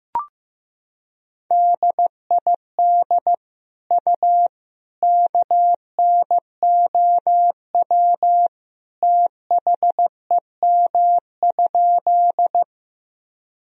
Morse code, heard as a keyed tone, about 15 wpm, 700 Hz. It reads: DID U KNOW THEM?
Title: Sentences from Top 100 Words and Abbreviations 15wpm